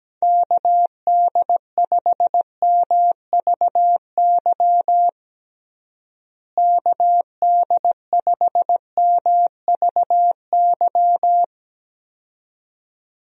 Morse code, heard as a keyed tone, about 17 wpm, 700 Hz. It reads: KD5MVY KD5MVY
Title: USA Call Signs - 6 Characters - Repeated 2x - 17wpm